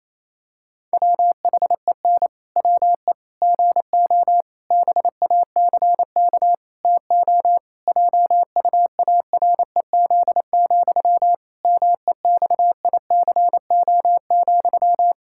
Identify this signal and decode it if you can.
Morse code, heard as a keyed tone, about 28 wpm, 700 Hz. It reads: WHEN WE GO BACK TO JUAREZ, MEXICO,
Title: Pangram Challenge v2 - 28wpm